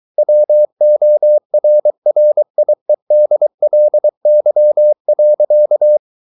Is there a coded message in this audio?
Yes: Morse code from 0.2 to 6.0 s, heard as a keyed tone, about 23 wpm, 600 Hz. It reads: WORRIEDLY.